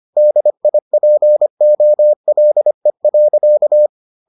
Morse code, heard as a keyed tone, about 25 wpm, 600 Hz. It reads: DIPOLE.